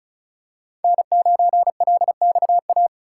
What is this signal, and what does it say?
Morse code, heard as a keyed tone, about 34 wpm, 700 Hz. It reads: N9LXA
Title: USA Call Signs - 5 Characters - 4x Character Spacing - 35wpm